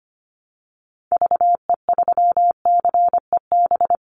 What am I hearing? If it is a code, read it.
Morse code, heard as a keyed tone, about 25 wpm, 700 Hz. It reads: VE3CEB
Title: Call Signs - 25wpm